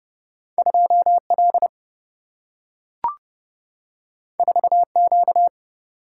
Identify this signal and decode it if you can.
Morse code, heard as a keyed tone, about 30 wpm, 700 Hz. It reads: 2 L 4 Q
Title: Single-Letter Single-Number 30wpm